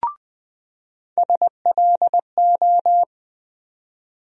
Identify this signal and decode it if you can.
Morse code, heard as a keyed tone, about 20 wpm, 700 Hz. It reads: SLO
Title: USA Call Sign Postfixes - 3 Letters 20wpm